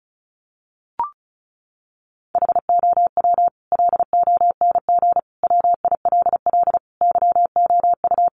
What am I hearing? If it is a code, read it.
Morse code, heard as a keyed tone, about 34 wpm, 700 Hz. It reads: HOW LONG WILL YOU
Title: Sentences from Top 1200 Words 35wpm